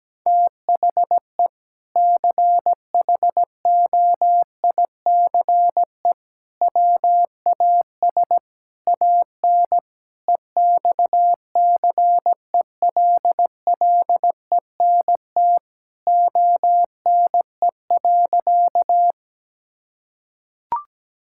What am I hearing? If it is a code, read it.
Morse code, heard as a keyed tone, about 17 wpm, 700 Hz. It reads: THE CHOICE WAS AN EXCELLENT ONE.
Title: Sentences from Top 1000 Words - Review Part 6 - 17wpm